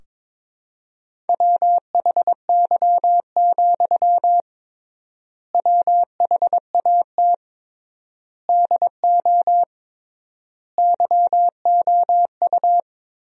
Morse code, heard as a keyed tone, about 22 wpm, 700 Hz. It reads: WHY, WHAT DO YOU
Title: 3x Word-Spacing - Sentences from Top 200 Words - 22wpm